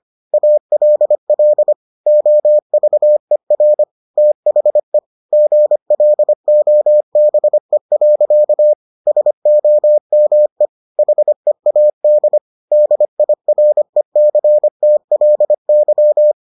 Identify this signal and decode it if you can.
Morse code, heard as a keyed tone, about 25 wpm, 600 Hz. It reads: ALL OVER THE GLOBE. SOME HEAD DIRECTLY